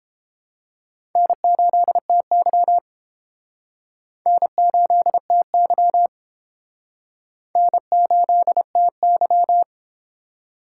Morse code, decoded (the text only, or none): N8TY N8TY N8TY